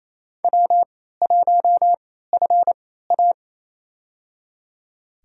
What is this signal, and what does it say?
Morse code, heard as a keyed tone, about 28 wpm, spaced as for 16 wpm, 700 Hz. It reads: W1FA
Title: USA Call Signs - 4 Characters - 3x Character Spacing - 28wpm